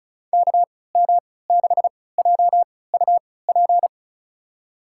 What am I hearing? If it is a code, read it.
Morse code, heard as a keyed tone, about 36 wpm, spaced as for 20 wpm, 700 Hz. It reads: KM6JUP
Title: USA Call Signs - 6 Characters - 3x Character Spacing - 35wpm